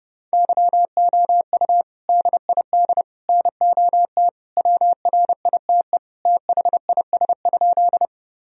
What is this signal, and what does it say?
Morse code, heard as a keyed tone, about 30 wpm, 700 Hz. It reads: YOU DID NOT WRITE THIS?
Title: Sentences from Top 1000 Words - Review Part 1 - 30wpm